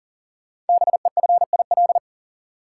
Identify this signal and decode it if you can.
Morse code, heard as a keyed tone, about 42 wpm, 690 Hz. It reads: BEFIL
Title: Code Groups - 5 Letters - 40wpm